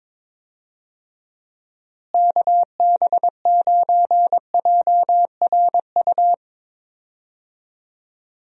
Morse code, heard as a keyed tone, about 21 wpm, 700 Hz. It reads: KB9JRU
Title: USA Call Signs - 6 Characters - Repeated 2x - 22wpm